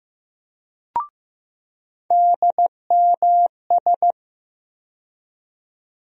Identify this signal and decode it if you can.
Morse code, heard as a keyed tone, about 15 wpm, 700 Hz. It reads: DMS
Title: USA Call Sign Postfixes - 3 Letters 15wpm